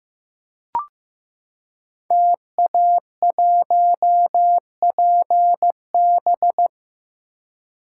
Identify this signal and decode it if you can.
Morse code, heard as a keyed tone, about 15 wpm, 700 Hz. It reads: TA1PB
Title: Code Groups - 5 Letter:Numbers - 15wpm